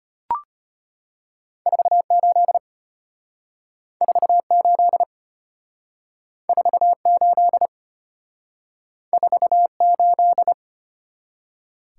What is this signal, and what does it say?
Morse code, heard as a keyed tone, about 33 wpm, 700 Hz. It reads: 48 48 48 48